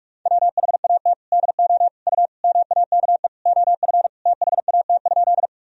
Morse code, heard as a keyed tone, about 46 wpm, 700 Hz. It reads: WHAT DO U MAKE OF THAT?
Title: Sentences from Top 100 Words and Abbreviations 45wpm